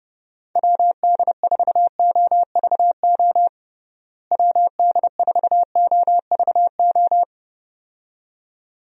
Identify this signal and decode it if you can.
Morse code, heard as a keyed tone, about 30 wpm, 700 Hz. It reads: WD4OVO WD4OVO
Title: USA Call Signs - 6 Characters - Repeated 2x - 30wpm